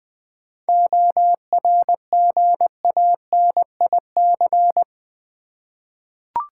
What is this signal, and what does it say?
Morse code, heard as a keyed tone, about 20 wpm, 700 Hz. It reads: ORGANIC
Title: 7 Letter Words - First 3 Letters Given - 20wpm